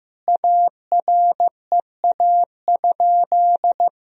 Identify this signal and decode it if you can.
Morse code, heard as a keyed tone, about 15 wpm, 700 Hz. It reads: AREA?
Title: Sentences from Top 1000 Words - Review Part 2 - 15wpm